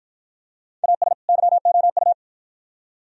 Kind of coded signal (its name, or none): Morse code